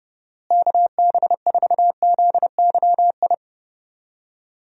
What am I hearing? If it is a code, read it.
Morse code, heard as a keyed tone, about 30 wpm, 700 Hz. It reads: KB4ZYI
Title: USA Call Signs - 6 Characters - Repeated 2x - 30wpm